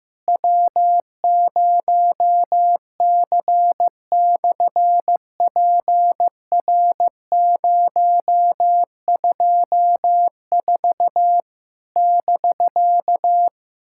Morse code, heard as a keyed tone, about 15 wpm, 700 Hz. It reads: W0C/PR024 <BK>
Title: SOTA QSO Practice 15wpm